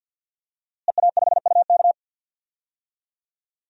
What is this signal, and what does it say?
Morse code, heard as a keyed tone, about 48 wpm, 700 Hz. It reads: EA5UK
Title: DXCC-100 Call Signs - 5 Characters - Repeated 3x - 50wpm